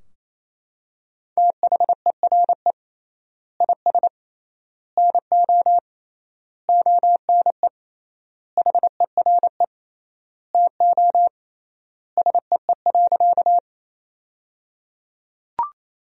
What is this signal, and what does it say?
Morse code, heard as a keyed tone, about 28 wpm, 700 Hz. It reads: THERE IS NO ONE HERE TO SEE.
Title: 3x Word-Spacing - Sentences from Top 200 Words - 28wpm